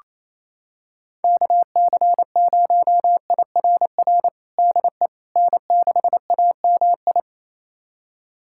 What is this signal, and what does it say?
Morse code, heard as a keyed tone, about 28 wpm, 700 Hz. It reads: KC0IRR DE N6AMI
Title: USA Callsign DE Callsign 28wpm